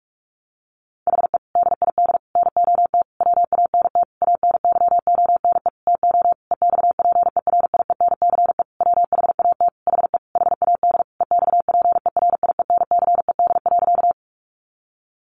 Morse code, heard as a keyed tone, about 44 wpm, 700 Hz. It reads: HE DID NOT WANT ANYONE TO EXPERIENCE WHAT HE HAD EXPERIENCED.
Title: Sentences from Top 1100 Words 45wpm